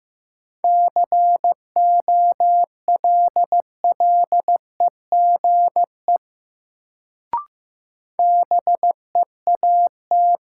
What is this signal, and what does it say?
Morse code, heard as a keyed tone, about 15 wpm, 700 Hz. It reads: COLLEGE BEAT
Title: Sentences from Top 1300 Words - New Word Drill - 15wpm